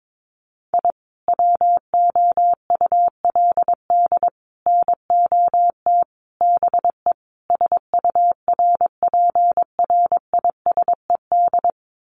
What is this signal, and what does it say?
Morse code, heard as a keyed tone, about 22 wpm, 700 Hz. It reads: I WOULD NOT BE SURPRISED
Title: Sets of 5 Words Encore2 22wpm